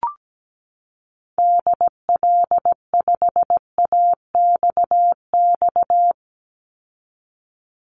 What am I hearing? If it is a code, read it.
Morse code, heard as a keyed tone, about 17 wpm, 700 Hz. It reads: DL5AXX
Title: Call Signs - 17wpm